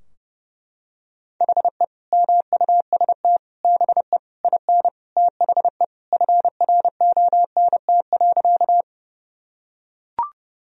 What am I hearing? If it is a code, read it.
Morse code, heard as a keyed tone, about 30 wpm, 700 Hz. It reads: HE MUST BE IN THE FRONT.
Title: Sentences from Top 1000 Words - Review Part 2 - 30wpm